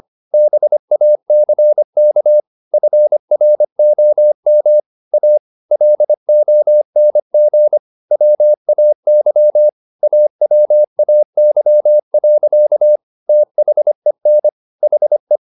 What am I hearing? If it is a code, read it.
Morse code, heard as a keyed tone, about 25 wpm, 600 Hz. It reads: BACK FROM A LONG WAY AWAY. THEN HE
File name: the old man and the sea part 1 25wpm